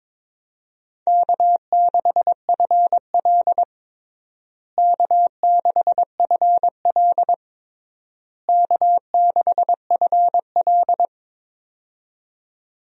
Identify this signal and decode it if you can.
Morse code, heard as a keyed tone, about 22 wpm, 700 Hz. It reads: K6FL K6FL K6FL
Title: USA Call Signs - 4 Characters - Repeated 3x - 22wpm